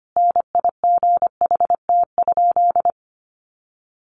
Morse code, heard as a keyed tone, about 25 wpm, 700 Hz. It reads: NIGHT?
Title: Sentences from Top 700 Words 25wpm